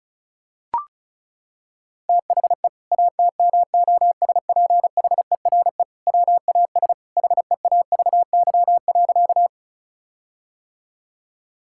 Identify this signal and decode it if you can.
Morse code, heard as a keyed tone, about 36 wpm, 700 Hz. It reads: THE ATMOSPHERE WAS HEAVY.
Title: Sentences from Top 1600 Words 35wpm